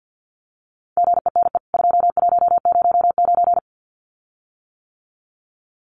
Morse code, heard as a keyed tone, about 49 wpm, 700 Hz. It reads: GENE 2109
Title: CWT Exchange 50wpm